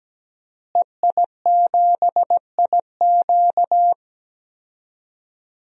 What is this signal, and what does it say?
Morse code, heard as a keyed tone, about 17 wpm, 700 Hz. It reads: EI7IQ
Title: DXCC-100 Call Signs - 5 Characters - Repeated 3x - 17wpm